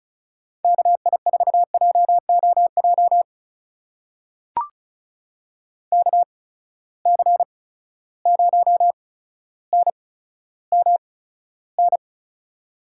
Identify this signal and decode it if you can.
Morse code, heard as a keyed tone, about 36 wpm, spaced as for 15 wpm, 700 Hz. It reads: KI4JOJ KC0NMN
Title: USA Call Signs - 6 Characters - 8x Character Spacing - 35wpm